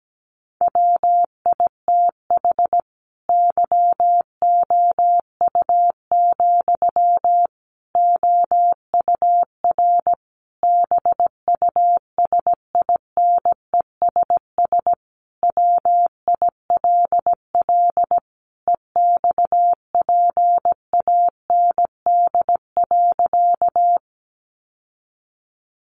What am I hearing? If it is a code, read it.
Morse code, heard as a keyed tone, about 17 wpm, 700 Hz. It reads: WITH YOU, OUR BUSINESS WILL EXPAND.
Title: Sentences from Top 1400 Words 17wpm